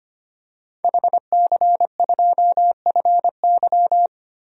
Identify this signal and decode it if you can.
Morse code, heard as a keyed tone, about 25 wpm, 700 Hz. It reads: HC2FY
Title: DXCC-100 Call Signs - 5 Characters - Repeated 2x - 25wpm